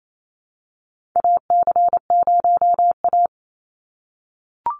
Morse code, heard as a keyed tone, about 28 wpm, 700 Hz. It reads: AC0A
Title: NA SOTA Chaser Callsigns 28wpm